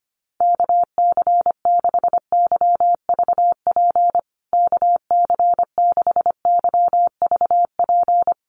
Morse code, heard as a keyed tone, about 25 wpm, 700 Hz. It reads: KC6YVP KC6YVP
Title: USA Call Signs Repeated 3x 25wpm